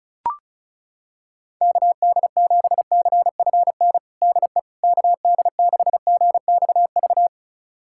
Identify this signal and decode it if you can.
Morse code, heard as a keyed tone, about 36 wpm, 700 Hz. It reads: KD7CFN DE KD6GXV
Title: USA Callsign DE Callsign 35wpm